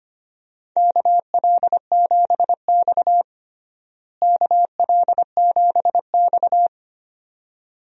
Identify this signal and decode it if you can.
Morse code, heard as a keyed tone, about 25 wpm, 700 Hz. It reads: KL7X KL7X
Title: DXCC-100 Call Signs - 4 Characters - Repeated 2x - 25wpm